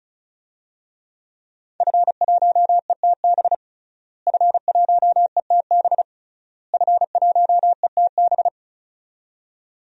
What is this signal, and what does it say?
Morse code, heard as a keyed tone, about 36 wpm, 700 Hz. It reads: F1ETB F1ETB F1ETB
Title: DXCC-100 Call Signs - 5 Characters - Repeated 3x - 35wpm